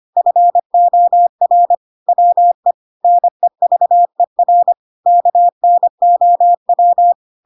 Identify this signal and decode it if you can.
Morse code, heard as a keyed tone, about 25 wpm, 700 Hz. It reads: FOR WE NEVER KNOW
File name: Pinocchio / Pinocchio ch7 25 25 ik1hge